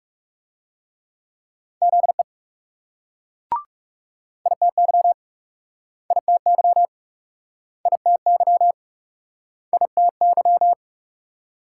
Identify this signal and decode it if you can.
Morse code, heard as a keyed tone, about 39 wpm, 700 Hz. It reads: ZE ITY ITY ITY ITY